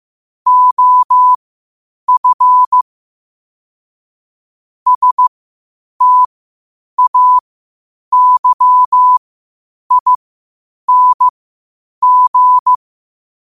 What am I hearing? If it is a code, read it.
Morse code, heard as a keyed tone, about 15 wpm, spaced as for 8 wpm, 1000 Hz. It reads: OF STAYING
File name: qotdfiles / 8wpm